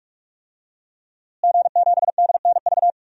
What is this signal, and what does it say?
Morse code, heard as a keyed tone, about 42 wpm, 700 Hz. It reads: G7DNV
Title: DXCC-100 Call Signs - 5 Characters - Repeated 3x - 45wpm